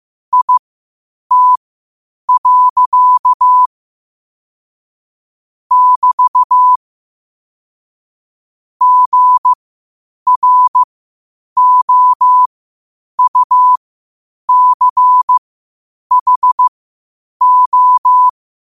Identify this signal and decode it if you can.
Morse code, heard as a keyed tone, about 15 wpm, spaced as for 8 wpm, 1000 Hz. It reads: IT. <BT> GROUCHO
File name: qotdfiles / 8wpm